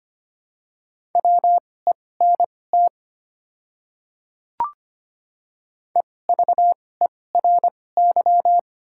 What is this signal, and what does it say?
Morse code, heard as a keyed tone, about 25 wpm, spaced as for 18 wpm, 700 Hz. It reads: WENT EVERY
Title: 2x Character Spacing - Top 300 Words - 25wpm